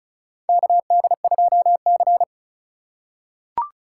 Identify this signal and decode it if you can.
Morse code, heard as a keyed tone, about 34 wpm, 700 Hz. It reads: KD2C
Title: USA Call Signs - 3x Character Spacing - 35wpm